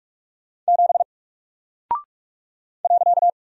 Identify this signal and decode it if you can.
Morse code, heard as a keyed tone, about 45 wpm, 700 Hz. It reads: 7 .
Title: Single Character (Letter:Number:Punctuation:Prosign) Alt 45wpm